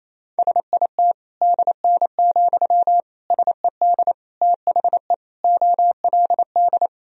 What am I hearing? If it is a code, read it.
Morse code, heard as a keyed tone, about 28 wpm, 700 Hz. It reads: SIT DN, SED THE OLD
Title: Sentences from Top 500 Words and Abbreviations 28wpm